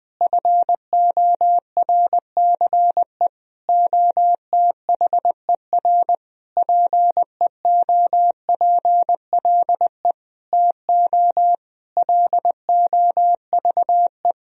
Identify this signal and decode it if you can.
Morse code, heard as a keyed tone, about 21 wpm, 700 Hz. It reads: FORCE OTHER PEOPLE TO LOVE
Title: Sentences from Top 1000 Words - Review Part 3 - 20wpm